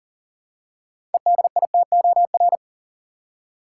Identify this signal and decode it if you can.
Morse code, heard as a keyed tone, about 38 wpm, 700 Hz. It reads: EDITOR
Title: Six Letter Words - 40wpm